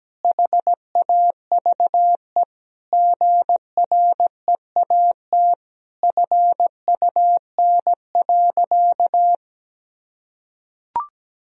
Morse code, heard as a keyed tone, about 17 wpm, 700 Hz. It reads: HAVE GREAT FUN.
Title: Sentences from Top 1000 Words - Review Part 4 - 17wpm